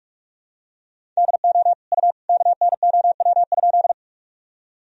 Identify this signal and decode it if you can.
Morse code, heard as a keyed tone, about 46 wpm, 700 Hz. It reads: DO U KNOW?
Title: Sentences from Top 100 Words and Abbreviations 45wpm